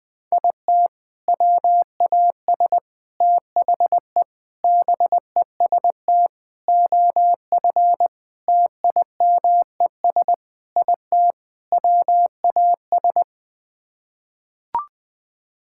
Morse code, heard as a keyed tone, about 21 wpm, 700 Hz. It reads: IT WAS THE BEST OF TIMES IT WAS
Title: Sets of 8 Words - 20wpm